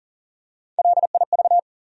Morse code, heard as a keyed tone, about 40 wpm, 700 Hz. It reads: LIV
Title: USA Common Names - Max 3 Letters - 40wpm